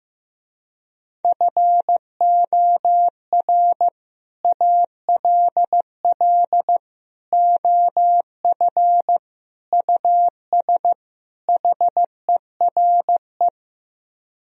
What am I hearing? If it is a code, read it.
Morse code, heard as a keyed tone, about 16 wpm, 700 Hz. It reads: FOR ALL OF US HERE
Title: Speed-Racing - Sets of 5 Words 15wpm